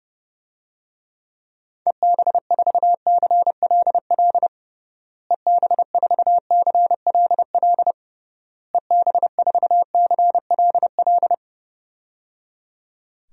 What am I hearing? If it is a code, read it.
Morse code, heard as a keyed tone, about 30 wpm, 700 Hz. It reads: EB4CLL EB4CLL EB4CLL
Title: DXCC-100 Call Signs - 6 Characters - Repeated 3x - 30wpm